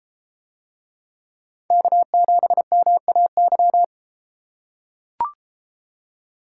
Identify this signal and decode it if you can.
Morse code, heard as a keyed tone, about 33 wpm, 700 Hz. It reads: K7MAY